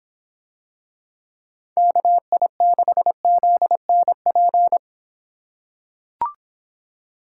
Morse code, heard as a keyed tone, about 26 wpm, 700 Hz. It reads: KI6ZNP